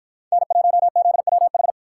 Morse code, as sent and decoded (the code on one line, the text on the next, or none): -. .---- --.. .-- ....
N1ZWH